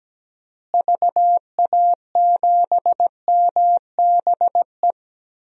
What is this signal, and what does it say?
Morse code, heard as a keyed tone, about 17 wpm, 700 Hz. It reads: VA7MBE
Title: DXCC-100 Call Signs - 6 Characters - Repeated 2x - 17wpm